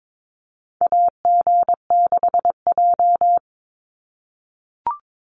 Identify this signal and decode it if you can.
Morse code, heard as a keyed tone, about 22 wpm, 700 Hz. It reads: AG6J
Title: USA Call Signs 22wpm